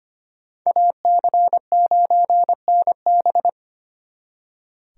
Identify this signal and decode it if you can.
Morse code, heard as a keyed tone, about 33 wpm, 700 Hz. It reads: AC9NB